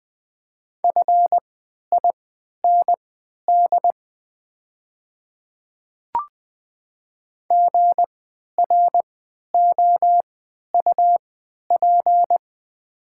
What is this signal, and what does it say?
Morse code, heard as a keyed tone, about 21 wpm, spaced as for 11 wpm, 690 Hz. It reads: FIND GROUP
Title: 3x Character Spacing - Top 300 Words - 20wpm